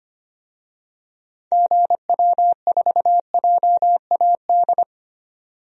Morse code, heard as a keyed tone, about 25 wpm, 700 Hz. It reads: GW4JAD
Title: DXCC-100 Call Signs - 6 Characters - 25wpm